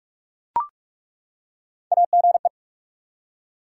Morse code, heard as a keyed tone, about 41 wpm, 700 Hz. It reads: AGE
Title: Three Letter Words - 45wpm